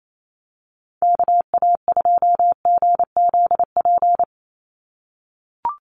Morse code, heard as a keyed tone, about 28 wpm, 700 Hz. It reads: KA2GZP
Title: USA Call Signs 28wpm